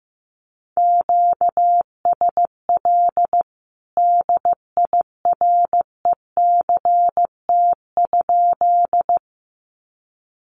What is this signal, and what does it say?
Morse code, heard as a keyed tone, about 15 wpm, 700 Hz. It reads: QSL DIRECT?
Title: Common QSO Phrases 15wpm